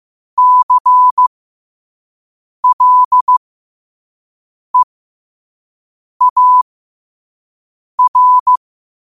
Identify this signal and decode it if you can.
Morse code, heard as a keyed tone, about 15 wpm, spaced as for 5 wpm, 1000 Hz. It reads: CLEAR
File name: qotdfiles / 5wpm